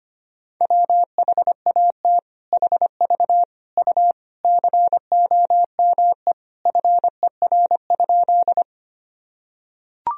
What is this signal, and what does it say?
Morse code, heard as a keyed tone, about 25 wpm, 700 Hz. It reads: WHAT HV U COME FER?
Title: Sentences from Top 100 Words and Abbreviations 25wpm